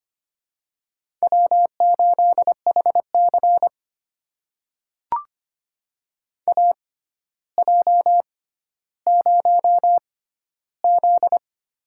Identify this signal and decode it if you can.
Morse code, heard as a keyed tone, about 25 wpm, spaced as for 13 wpm, 700 Hz. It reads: W8HC AJ0Z
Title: USA Call Signs - 4 Characters - 6x Character Spacing - 25wpm